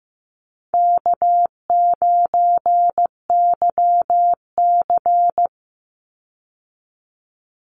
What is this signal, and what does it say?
Morse code, heard as a keyed tone, about 15 wpm, 700 Hz. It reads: K9YC
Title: Call Signs Repeated 3x - 15wpm